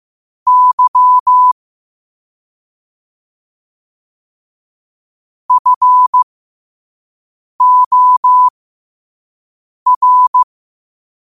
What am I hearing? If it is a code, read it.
Morse code, heard as a keyed tone, about 15 wpm, spaced as for 5 wpm, 1000 Hz. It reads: Y FOR
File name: qotdfiles / 5wpm